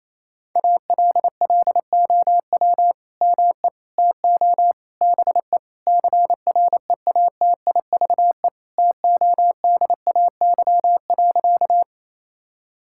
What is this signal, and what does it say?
Morse code, heard as a keyed tone, about 28 wpm, 700 Hz. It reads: ALLOW ME TO BE CREATIVE TODAY.
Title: Sentences from Top 1000 Words - Review Part 5 - 28wpm